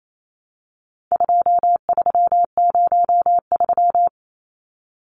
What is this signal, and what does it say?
Morse code, heard as a keyed tone, about 28 wpm, 700 Hz. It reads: 2303
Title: Numbers - 4-Digit 28wpm